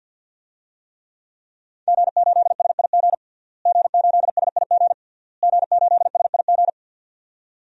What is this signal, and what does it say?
Morse code, heard as a keyed tone, about 51 wpm, 700 Hz. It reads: G8SIG G8SIG G8SIG
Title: DXCC-100 Call Signs - 5 Characters - Repeated 3x - 50wpm